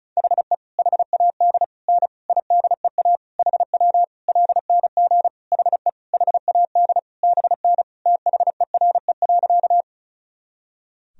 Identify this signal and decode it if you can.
Morse code, heard as a keyed tone, about 34 wpm, 700 Hz. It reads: HE HAD N IDEA HW LNG HE HAD BN THERE.
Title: Sentences from Top 300 Words and Abbreviations 35wpm